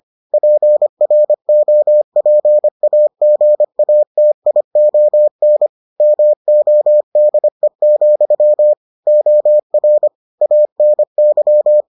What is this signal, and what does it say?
Morse code, heard as a keyed tone, about 25 wpm, 600 Hz. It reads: PROPAGATION MODE, OR ANY